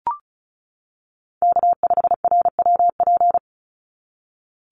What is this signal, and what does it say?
Morse code, heard as a keyed tone, about 34 wpm, 700 Hz. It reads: K5RWP
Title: NA SOTA Chaser Callsigns 35wpm